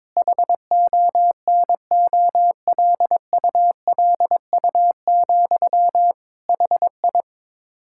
Morse code, heard as a keyed tone, about 22 wpm, 700 Hz. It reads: HONOLULU, HI
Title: K1USN SST QSO Practice 22wpm